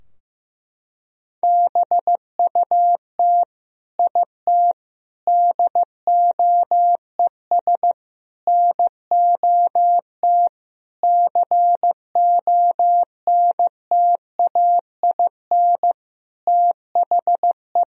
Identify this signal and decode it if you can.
Morse code, heard as a keyed tone, about 15 wpm, 700 Hz. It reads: BUT IT DOES NOT CONTAIN THE
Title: Sentences from Top 1000 Words - Review Part 2 - 15wpm